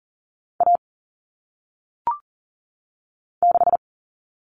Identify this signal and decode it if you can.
Morse code, heard as a keyed tone, about 40 wpm, 700 Hz. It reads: A 6
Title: Common Cut Numbers - 1-Digit 40wpm